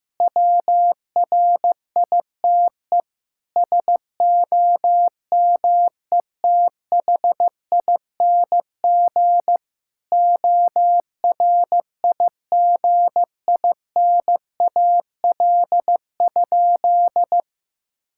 Morse code, heard as a keyed tone, about 15 wpm, 700 Hz. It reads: WRITE SOMETHING ORIGINAL?
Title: Sentences from Top 1000 Words - Review Part 6 - 15wpm